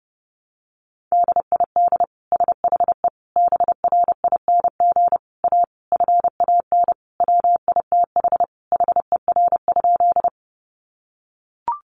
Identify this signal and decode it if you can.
Morse code, heard as a keyed tone, about 30 wpm, 700 Hz. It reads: DID SHE BRING A FAN WITH HER?
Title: Sentences from Top 1300 Words 30wpm